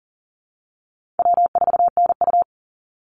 Morse code, heard as a keyed tone, about 38 wpm, 700 Hz. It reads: W4NU
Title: Call Signs - 40wpm